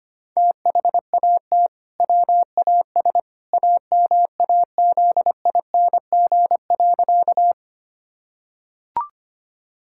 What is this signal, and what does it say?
Morse code, heard as a keyed tone, about 25 wpm, 700 Hz. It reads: THAT WAS AMAZING.
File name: practice / 25wpm / Sentences with 2-Word Phrases - 25wpm